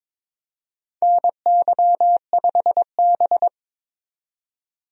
Morse code, heard as a keyed tone, about 21 wpm, 700 Hz. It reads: NY5B
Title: USA Call Signs - 4 Characters - 22wpm